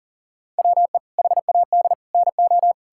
Morse code, heard as a keyed tone, about 40 wpm, 700 Hz. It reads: WE HAD NO
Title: Sentences from Top 1000 Words - Review Part 6 - 40wpm